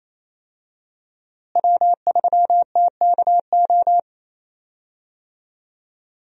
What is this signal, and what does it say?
Morse code, heard as a keyed tone, about 27 wpm, 700 Hz. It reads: W3TKO
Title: USA Call Signs - 5 Characters - Repeated 3x - 28wpm